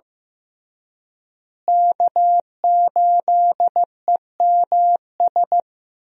Morse code, heard as a keyed tone, about 15 wpm, 700 Hz. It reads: K8EMS
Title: USA Call Signs - 5 Characters - Repeated 3x - 15wpm